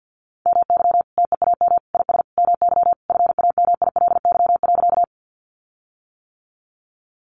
Morse code, heard as a keyed tone, about 50 wpm, 700 Hz. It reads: MY TEAM IS MY FAMILY.
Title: Sentences from Top 1500 Words 50wpm